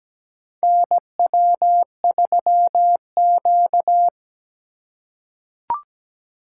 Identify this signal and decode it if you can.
Morse code, heard as a keyed tone, about 17 wpm, 700 Hz. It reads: NW3Q
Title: USA Call Signs - 4 Characters - 4x Character Spacing - 17wpm